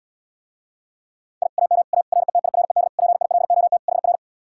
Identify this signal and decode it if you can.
Morse code, heard as a keyed tone, about 80 wpm, 700 Hz. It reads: I AM A RETIRED PILOT HR